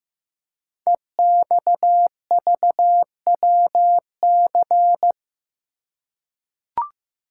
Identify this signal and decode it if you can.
Morse code, heard as a keyed tone, about 15 wpm, 700 Hz. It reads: EXVWC
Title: Code Groups - 5 Letters - 15wpm